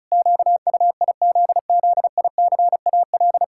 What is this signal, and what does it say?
Morse code, heard as a keyed tone, about 36 wpm, 700 Hz. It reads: QUIZZICAL
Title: Pangram Challenge v2 - 35wpm